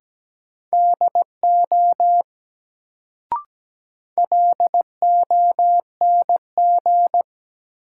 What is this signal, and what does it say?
Morse code, heard as a keyed tone, about 17 wpm, 700 Hz. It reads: DO LONG
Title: Sets of 2 Words - New Word Drill - 17wpm